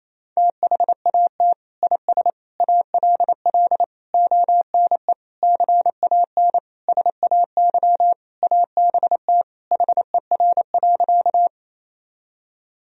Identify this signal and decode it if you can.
Morse code, heard as a keyed tone, about 28 wpm, 700 Hz. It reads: THAT IS ALL ONE CAN SAY ABT HER.
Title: Sentences from Top 200 Words and Abbreviations 28wpm